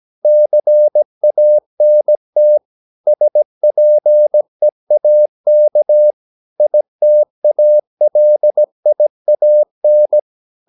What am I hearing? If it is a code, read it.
Morse code, heard as a keyed tone, about 17 wpm, 600 Hz. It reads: CANT SPEAK ITALIAN
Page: Romanen Never af Ken Follet på engelsk.